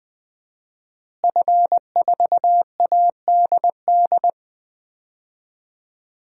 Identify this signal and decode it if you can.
Morse code, heard as a keyed tone, about 20 wpm, 700 Hz. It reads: F4ADD
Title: DXCC-100 Call Signs - 5 Characters - Repeated 2x - 20wpm